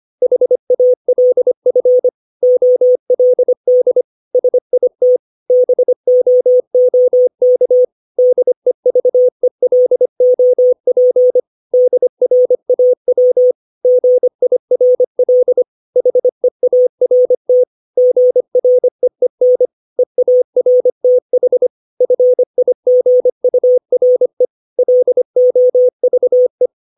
Morse code, heard as a keyed tone, about 25 wpm, 500 Hz. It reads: HALF OLD SIT BOOK DEVELOP DRAW GIRL HEART GREEN EARTH FIGURE LOVE
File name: 632 words / 632 Words 25WPM